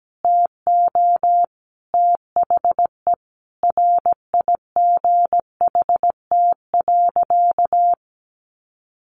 Morse code, heard as a keyed tone, about 17 wpm, 700 Hz. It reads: TO THE RIGHT.
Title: Sentences from Top 300 Words 17wpm